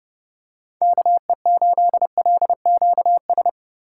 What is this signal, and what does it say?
Morse code, heard as a keyed tone, about 31 wpm, 700 Hz. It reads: KE8LQS